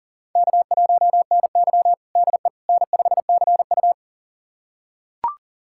Morse code, heard as a keyed tone, about 37 wpm, 700 Hz. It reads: K1NY DE N5CU
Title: USA Callsign DE Callsign - 4 Characters - 40wpm